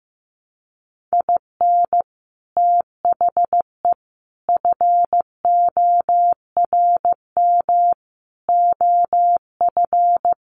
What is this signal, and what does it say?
Morse code, heard as a keyed tone, about 15 wpm, 700 Hz. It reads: IN THE FORM OF
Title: Sets of 5 Words 15wpm